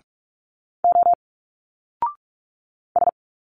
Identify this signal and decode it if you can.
Morse code, heard as a keyed tone, about 42 wpm, 700 Hz. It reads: O S
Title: Single Letters 45wpm